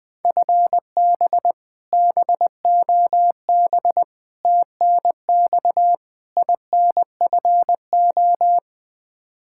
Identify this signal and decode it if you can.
Morse code, heard as a keyed tone, about 21 wpm, 700 Hz. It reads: FB BOB TNX INFO
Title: K1USN SST QSO Practice 20wpm